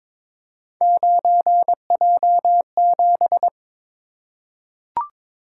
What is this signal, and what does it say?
Morse code, heard as a keyed tone, about 22 wpm, 700 Hz. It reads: 9J7
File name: practice / 22wpm / Three Letter-Number Combinations - 22wpm